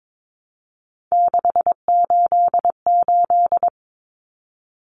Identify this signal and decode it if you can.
Morse code, heard as a keyed tone, about 22 wpm, 700 Hz. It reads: 688
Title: Numbers - 3-Digit 22wpm